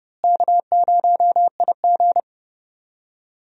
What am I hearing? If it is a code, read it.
Morse code, heard as a keyed tone, about 31 wpm, 700 Hz. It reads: K0IG
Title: USA Call Signs - 4 Characters - Repeated 2x - 30wpm